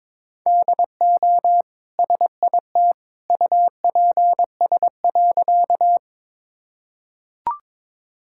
Morse code, heard as a keyed tone, about 22 wpm, 700 Hz. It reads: DO SIT UPS.
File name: practice / 22wpm / Easy Sentences with Hints - 22wpm